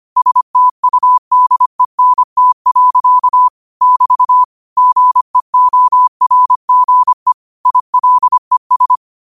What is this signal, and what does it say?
Morse code, heard as a keyed tone, about 25 wpm, 1000 Hz. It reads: STUDENT. <BT> GEORGE ILES
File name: qotdfiles / 25wpm